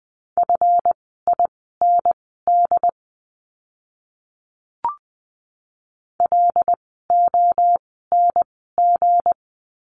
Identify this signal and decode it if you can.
Morse code, heard as a keyed tone, about 20 wpm, spaced as for 15 wpm, 700 Hz. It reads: FIND LONG
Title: 160 Lesson 26f - Letter g - Words 20wpm